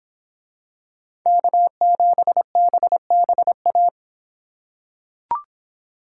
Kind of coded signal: Morse code

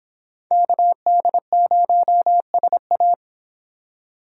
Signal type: Morse code

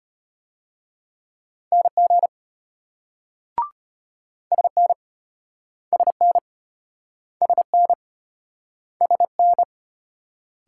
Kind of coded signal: Morse code

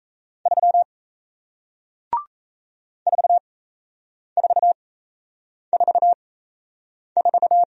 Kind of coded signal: Morse code

